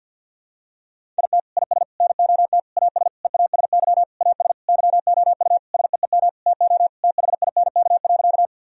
Morse code, heard as a keyed tone, about 51 wpm, 700 Hz. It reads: IT IS NOT AS EASY AS YOU SEEM TO THINK.
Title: Sentences from Top 1000 Words - Review Part 1 - 50wpm